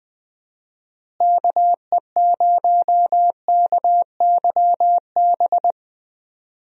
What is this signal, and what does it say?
Morse code, heard as a keyed tone, about 20 wpm, 700 Hz. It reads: KE0KYB
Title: USA Call Signs - 6 Characters - Repeated 3x - 20wpm